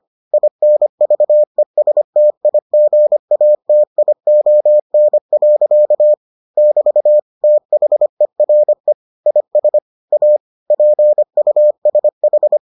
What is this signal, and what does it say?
Morse code, heard as a keyed tone, about 25 wpm, 600 Hz. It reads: INVESTIGATION. <BT> THERE IS A PUSH